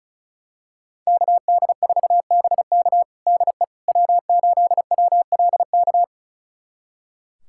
Morse code, heard as a keyed tone, about 34 wpm, 700 Hz. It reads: KD4BK DE W8WLK
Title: USA Callsign DE Callsign 35wpm